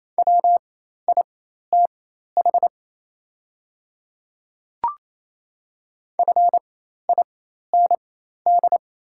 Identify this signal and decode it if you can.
Morse code, heard as a keyed tone, about 28 wpm, spaced as for 13 wpm, 690 Hz. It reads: WITH FIND
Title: 4x Character Spacing - Top 100 Words - 28wpm